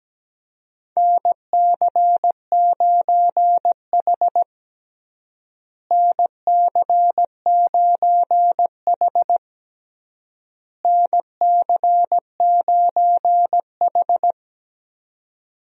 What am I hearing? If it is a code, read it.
Morse code, heard as a keyed tone, about 17 wpm, 700 Hz. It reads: NC9H NC9H NC9H
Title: USA Call Signs - 4 Characters - Repeated 3x - 17wpm